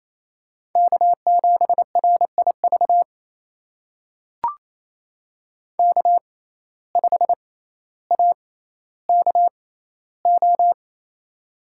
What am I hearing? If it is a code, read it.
Morse code, heard as a keyed tone, about 28 wpm, spaced as for 14 wpm, 700 Hz. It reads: K7RIV K5AKO
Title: USA Call Signs - 5 Characters - 6x Character Spacing - 28wpm